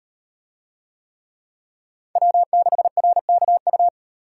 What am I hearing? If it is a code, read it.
Morse code, heard as a keyed tone, about 38 wpm, 700 Hz. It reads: W6RKU